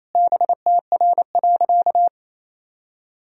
Morse code, heard as a keyed tone, about 27 wpm, 700 Hz. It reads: BTR.
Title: Sentences from Top 400 Words and Abbreviations 28wpm